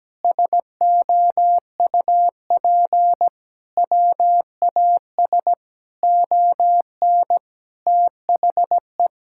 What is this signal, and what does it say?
Morse code, heard as a keyed tone, about 17 wpm, 700 Hz. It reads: SOUP WAS ON THE
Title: Sentences from Top 1000 Words - Review Part 6 - 17wpm